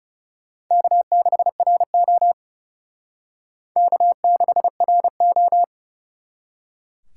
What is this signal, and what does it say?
Morse code, heard as a keyed tone, about 37 wpm, 700 Hz. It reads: K6RO K6RO
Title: Speed Racing - USA 1x2 Call Signs 30wpm